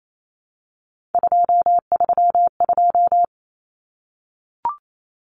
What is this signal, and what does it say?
Morse code, heard as a keyed tone, about 28 wpm, 700 Hz. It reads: 232
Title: Numbers - 3-Digit 28wpm